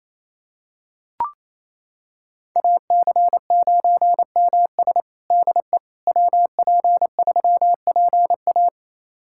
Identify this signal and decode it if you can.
Morse code, heard as a keyed tone, about 27 wpm, 700 Hz. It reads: AC9MS DE WP3PA
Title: USA Callsign DE Callsign - 5 Characters - 28wpm